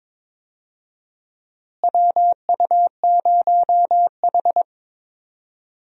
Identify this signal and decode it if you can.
Morse code, heard as a keyed tone, about 22 wpm, 700 Hz. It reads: WU0H
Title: USA Call Signs - 4 Characters - Repeated 2x - 22wpm